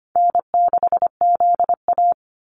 Morse code, heard as a keyed tone, about 25 wpm, 700 Hz. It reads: N6ZA
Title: NA SOTA Activator Callsigns 25wpm